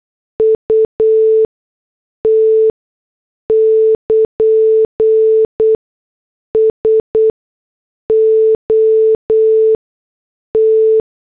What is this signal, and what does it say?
Morse code, heard as a keyed tone, about 8 wpm, spaced as for 6 wpm, 440 Hz. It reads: UT <KN> SOT
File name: morse / headlines